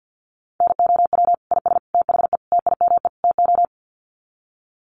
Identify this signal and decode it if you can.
Morse code, heard as a keyed tone, about 50 wpm, 700 Hz. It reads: NOW IS THE TIME TO
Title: Sets of 5 Words Encore2 50wpm